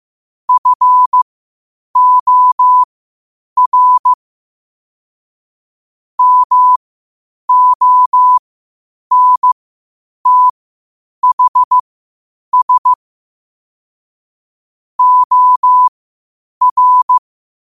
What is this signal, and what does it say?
Morse code, heard as a keyed tone, about 15 wpm, spaced as for 8 wpm, 1000 Hz. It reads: FOR MONTHS OR
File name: qotdfiles / 8wpm